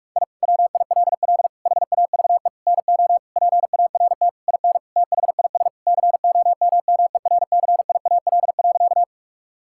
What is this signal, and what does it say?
Morse code, heard as a keyed tone, about 42 wpm, 700 Hz. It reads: I WILL HAVE NO PART IN THIS COMMERCIAL.
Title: Sentences from Top 1000 Words - Review Part 5 - 45wpm